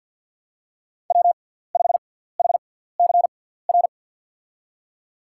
Morse code, heard as a keyed tone, about 49 wpm, spaced as for 17 wpm, 700 Hz. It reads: W5HCR
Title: USA Call Signs - 5 Characters - 6x Character Spacing - 50wpm